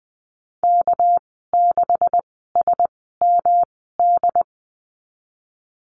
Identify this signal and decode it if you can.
Morse code, heard as a keyed tone, about 20 wpm, spaced as for 15 wpm, 700 Hz. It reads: K6SMD
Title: 200 Lesson 32g - Letter k - Call Signs 20wpm